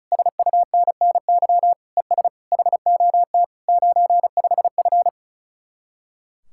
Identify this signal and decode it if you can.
Morse code, heard as a keyed tone, about 46 wpm, 700 Hz. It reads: SUNNY ES HOT 95F